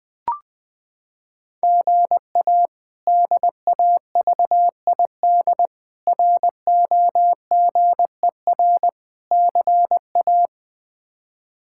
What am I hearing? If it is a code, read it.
Morse code, heard as a keyed tone, about 21 wpm, 700 Hz. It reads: GA DAVID ROGER CA
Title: K1USN SST QSO Practice - 20wpm